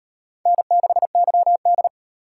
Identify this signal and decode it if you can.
Morse code, heard as a keyed tone, about 32 wpm, 700 Hz. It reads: N6YD